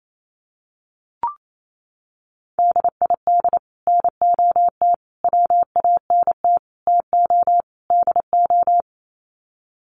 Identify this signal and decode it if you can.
Morse code, heard as a keyed tone, about 28 wpm, 700 Hz. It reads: DID NOT WANT TO DO
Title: Sets of 5 Words Encore 28wpm